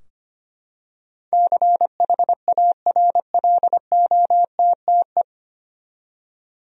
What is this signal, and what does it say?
Morse code, heard as a keyed tone, about 25 wpm, 700 Hz. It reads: CHARLOTTE
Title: USA Common Names - 25wpm